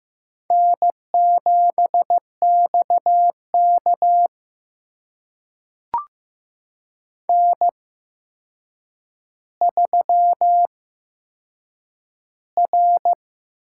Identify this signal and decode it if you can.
Morse code, heard as a keyed tone, about 15 wpm, spaced as for 7 wpm, 700 Hz. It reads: N7XK N3R
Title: USA Call Signs - 4 Characters - 8x Character Spacing - 15wpm